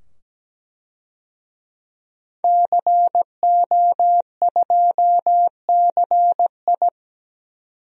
Morse code, heard as a keyed tone, about 17 wpm, 700 Hz. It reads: CO2CI
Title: DXCC-100 Call Signs - 5 Characters - 17wpm